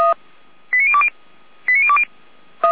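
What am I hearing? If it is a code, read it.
Five-tone selective call (ZVEI), heard as quick runs of five tones, tones of about 70 ms. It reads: 89020 89020
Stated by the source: ZVEI five-tone selective call received on SDR, from the Feuerwehr Orlamünde fire brigade server.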